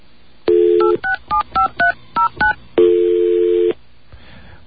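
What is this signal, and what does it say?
DTMF, heard as telephone touch tones, tones of about 120 ms. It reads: 8675309
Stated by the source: DTMF tones from the 2600 radio show Off The Hook, which discussed telephones and the 867-5309 number.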